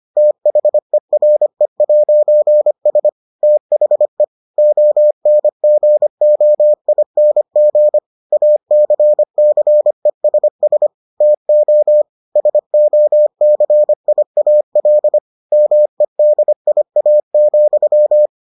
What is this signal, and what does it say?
Morse code, heard as a keyed tone, about 25 wpm, 600 Hz. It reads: THERE'S THE ONGOING ACCESS TO SOCIAL MEDIA,